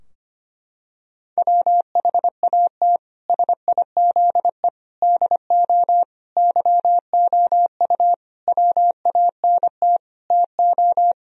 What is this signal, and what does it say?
Morse code, heard as a keyed tone, about 25 wpm, 700 Hz. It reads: WHAT SIZE DO YOU WANT TO
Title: Sentences from Top 1000 Words - Review Part 2 - 25wpm